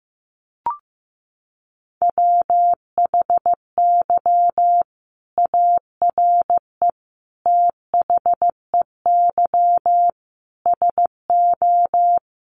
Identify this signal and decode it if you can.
Morse code, heard as a keyed tone, about 15 wpm, 700 Hz. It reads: WHY ARE THEY SO
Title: Sentences from Top 1200 Words 15wpm